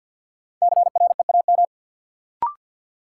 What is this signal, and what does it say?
Morse code, heard as a keyed tone, about 47 wpm, 700 Hz. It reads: CREAM
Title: Sentences from Top 1500 Words - New Word Drill Alt - 50wpm